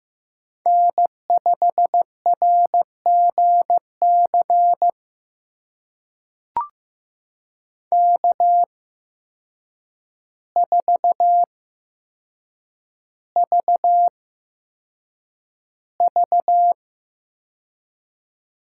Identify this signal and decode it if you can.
Morse code, heard as a keyed tone, about 15 wpm, spaced as for 7 wpm, 700 Hz. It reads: N5RGC K4VV
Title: USA Call Signs - 5 Characters - 8x Character Spacing - 15wpm